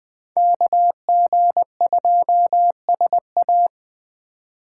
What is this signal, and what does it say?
Morse code, heard as a keyed tone, about 24 wpm, 700 Hz. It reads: KG2SA
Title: Speed Racing - USA 2x2 Call Signs 20wpm